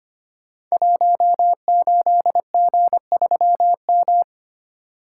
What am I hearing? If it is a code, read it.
Morse code, heard as a keyed tone, about 25 wpm, 700 Hz. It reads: 18G3M
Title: Code Groups - 5 Letter:Numbers - 25wpm